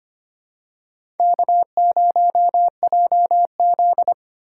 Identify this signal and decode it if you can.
Morse code, heard as a keyed tone, about 25 wpm, 700 Hz. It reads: K0JZ
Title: USA Call Signs - 4 Characters - 3x Character Spacing - 25wpm